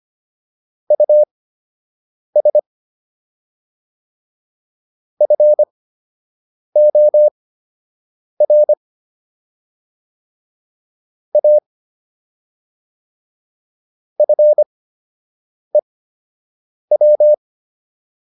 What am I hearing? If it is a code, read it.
Morse code, heard as a keyed tone, about 25 wpm, spaced as for 7 wpm, 600 Hz. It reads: US FOR A FEW